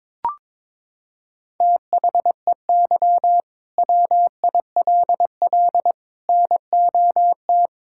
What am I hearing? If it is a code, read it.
Morse code, heard as a keyed tone, about 22 wpm, 700 Hz. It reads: THEY WILL NOT
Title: Sentences from Top 1000 Words - Review Part 5 - 22wpm